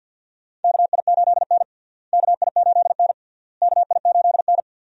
Morse code, heard as a keyed tone, about 47 wpm, 700 Hz. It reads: KI8N KI8N KI8N
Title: USA Call Signs - 4 Characters - Repeated 3x - 50wpm